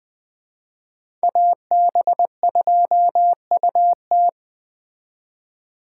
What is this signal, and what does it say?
Morse code, heard as a keyed tone, about 19 wpm, 700 Hz. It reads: AB2UT